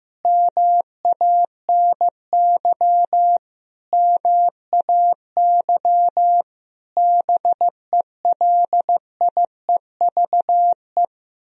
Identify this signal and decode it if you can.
Morse code, heard as a keyed tone, about 15 wpm, 700 Hz. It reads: MANY MAY BELIEVE
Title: Sentences from Top 1000 Words - Review Part 4 - 15wpm